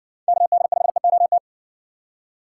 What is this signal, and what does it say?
Morse code, heard as a keyed tone, about 54 wpm, 700 Hz. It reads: KD5EOT